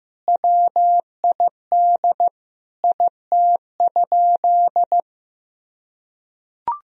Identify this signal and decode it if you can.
Morse code, heard as a keyed tone, about 16 wpm, 700 Hz. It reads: WID IT?
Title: Sentences from Top 100 Words and Abbreviations 15wpm